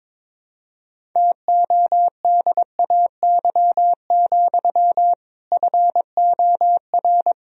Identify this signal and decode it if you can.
Morse code, heard as a keyed tone, about 22 wpm, 700 Hz. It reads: TODAY, FOR
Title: Sentences from Top 1000 Words - Review Part 1 - 22wpm